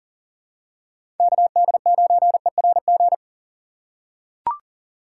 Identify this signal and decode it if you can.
Morse code, heard as a keyed tone, about 40 wpm, 700 Hz. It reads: KD9ERG
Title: USA Call Signs - 6 Characters - 4x Character Spacing - 40wpm